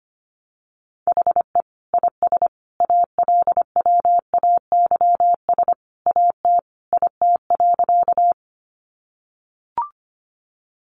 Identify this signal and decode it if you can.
Morse code, heard as a keyed tone, about 25 wpm, 700 Hz. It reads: HE IS ALWAYS AT IT.
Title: Sentences from Top 300 Words 25wpm